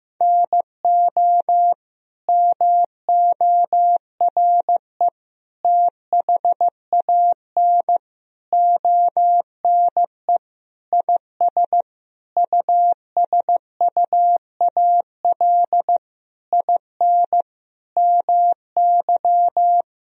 Morse code, heard as a keyed tone, about 15 wpm, 700 Hz. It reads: NO MORE THAN ONE IS USUAL IN MY
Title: Sentences from Top 1000 Words - Review Part 2 - 15wpm